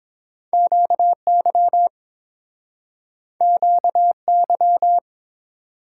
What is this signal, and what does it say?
Morse code, heard as a keyed tone, about 30 wpm, 700 Hz. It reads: QY QY